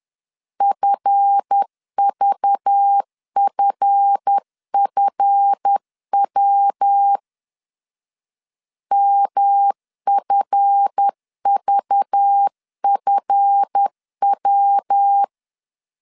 Morse code, heard as a keyed tone, about 10 wpm, 800 Hz. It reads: FVFFW MFVFW
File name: LEZIONI / 7 GF9Z / 7 GF 10wpm 2514